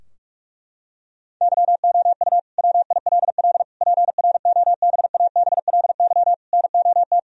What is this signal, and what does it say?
Morse code, heard as a keyed tone, about 47 wpm, 700 Hz. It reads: YOU WILL PROBABLY NOT
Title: Sentences from Top 1000 Words - Review Part 4 - 45wpm